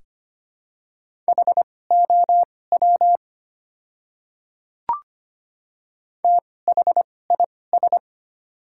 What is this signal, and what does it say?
Morse code, heard as a keyed tone, about 25 wpm, spaced as for 18 wpm, 700 Hz. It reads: HOW THIS
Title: 2x Character Spacing - Top 300 Words - 25wpm